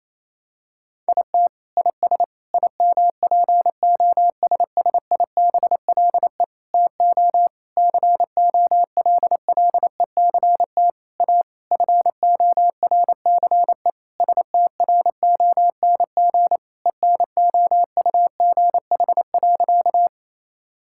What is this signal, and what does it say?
Morse code, heard as a keyed tone, about 28 wpm, 700 Hz. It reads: IT IS IMPOSSIBLE TO COLLECT A FORCE STRONG ENOUGH.
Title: Sentences from Top 1600 Words 28wpm